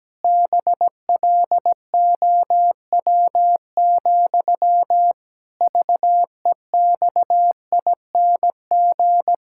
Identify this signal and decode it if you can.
Morse code, heard as a keyed tone, about 17 wpm, 700 Hz. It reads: BLOW, VEXING
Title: Pangram Challenge v2 - 17wpm